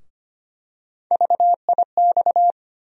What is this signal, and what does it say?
Morse code, heard as a keyed tone, about 25 wpm, 700 Hz. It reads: VIX
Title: USA Call Sign Postfixes - 3 Letters 25wpm